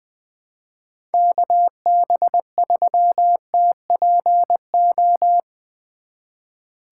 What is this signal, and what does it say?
Morse code, heard as a keyed tone, about 22 wpm, 700 Hz. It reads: KB3TPO
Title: Speed Racing - USA 2x3 Call Signs 20wpm